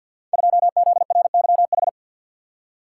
Morse code, heard as a keyed tone, about 52 wpm, 700 Hz. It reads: 2ZRYH
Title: Code Groups - 5 Letter:Numbers - 50wpm